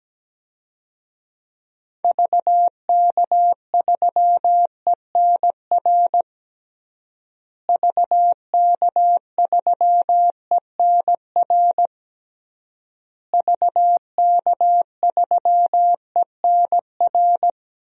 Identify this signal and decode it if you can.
Morse code, heard as a keyed tone, about 17 wpm, 700 Hz. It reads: VK3ENR VK3ENR VK3ENR
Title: DXCC-100 Call Signs - 6 Characters - Repeated 3x - 17wpm